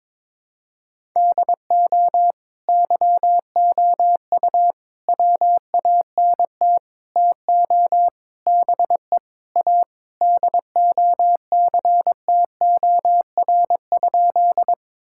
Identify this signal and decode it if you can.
Morse code, heard as a keyed tone, about 22 wpm, 700 Hz. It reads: DO YOU WANT TO BE A DOCTOR?
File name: practice / 22wpm / Sentences with 4-Word Phrases - 22wpm